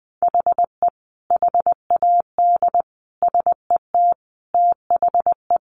Morse code, heard as a keyed tone, about 20 wpm, 700 Hz. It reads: HE HAD SET THE
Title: Sentences from Top 1500 Words 20wpm